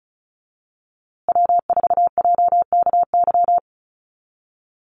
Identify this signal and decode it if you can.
Morse code, heard as a keyed tone, about 34 wpm, 700 Hz. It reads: W4JKY
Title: USA Call Signs Repeated 2x 35wpm